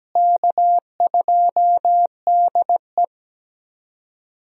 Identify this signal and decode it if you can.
Morse code, heard as a keyed tone, about 17 wpm, 700 Hz. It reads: K2DE
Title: USA Call Signs - 4 Characters - Repeated 2x - 17wpm